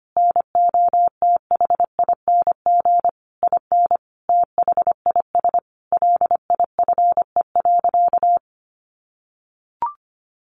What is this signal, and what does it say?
Morse code, heard as a keyed tone, about 25 wpm, 700 Hz. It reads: NOTHING IN THIS LIFE.
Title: Sentences from Top 1200 Words 25wpm